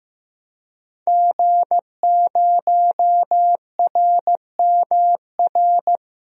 Morse code, heard as a keyed tone, about 15 wpm, 700 Hz. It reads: G0RMR
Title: DXCC-100 Call Signs - 5 Characters - Repeated 3x - 15wpm